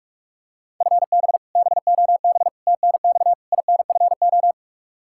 Morse code, heard as a keyed tone, about 41 wpm, 700 Hz. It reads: FB BOB TNX INFO
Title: Practice QSO - Exchange 3 - More Info - 45wpm